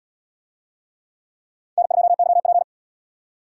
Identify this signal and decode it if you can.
Morse code, heard as a keyed tone, about 77 wpm, 700 Hz. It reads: N2YP